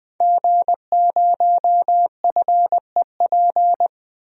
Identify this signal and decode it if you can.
Morse code, heard as a keyed tone, about 20 wpm, 700 Hz. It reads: G0FEP
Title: DXCC-100 Call Signs - 5 Characters - 20wpm